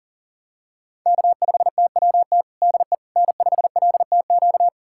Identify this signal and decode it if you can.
Morse code, heard as a keyed tone, about 37 wpm, 700 Hz. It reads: K5TWT DE N5LTQ
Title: USA Callsign DE Callsign - 5 Characters - 40wpm